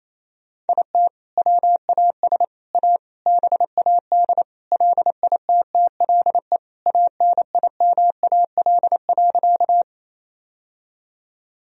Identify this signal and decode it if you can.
Morse code, heard as a keyed tone, about 28 wpm, 700 Hz. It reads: IT WAS A BAD LITTLE ANIMAL.
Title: Sentences from Top 1000 Words - Review Part 1 - 28wpm